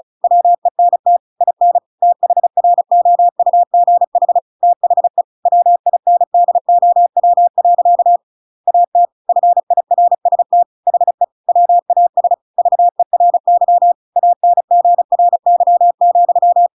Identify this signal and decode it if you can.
Morse code, heard as a keyed tone, about 34 wpm, 700 Hz. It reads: WENT IN THROUGH THE WINDOW. AT FIRST HE WAS VERY ANGRY,
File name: Pinocchio / Pinocchio ch7 35 35 ik1hge